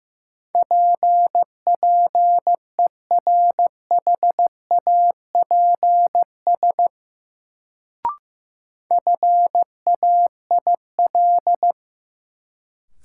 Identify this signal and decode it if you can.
Morse code, heard as a keyed tone, about 15 wpm, 700 Hz. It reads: PPERHAPS FAIL
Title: Sentences from Top 1100 Words - New Word Drill Alt - 15wpm